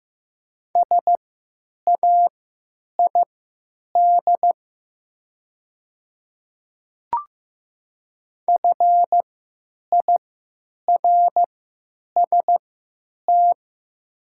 Morse code, heard as a keyed tone, about 15 wpm, spaced as for 9 wpm, 700 Hz. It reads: SAID FIRST
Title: 3x Character Spacing - Top 100 Words - 15wpm